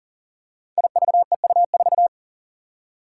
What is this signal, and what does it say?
Morse code, heard as a keyed tone, about 40 wpm, 700 Hz. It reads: IVEU4
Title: Five Challenging Letter-Number Combinations 40wpm